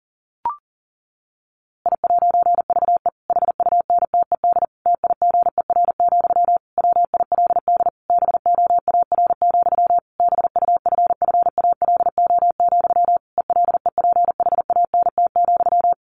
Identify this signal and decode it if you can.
Morse code, heard as a keyed tone, about 40 wpm, 700 Hz. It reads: I'VE HUNTED TIGER, WILD BOAR, BUFFALO, ELEPHANT,
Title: A Sound Of Thunder 40wpm